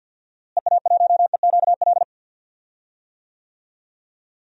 Morse code, heard as a keyed tone, about 53 wpm, 700 Hz. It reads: EA1EQL
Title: DXCC-100 Call Signs - 6 Characters - Repeated 2x - 50wpm